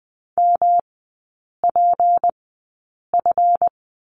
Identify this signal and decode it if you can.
Morse code, heard as a keyed tone, about 20 wpm, spaced as for 8 wpm, 700 Hz. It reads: MPF
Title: 152 Lesson 25e - Letter p - Three Characters 20wpm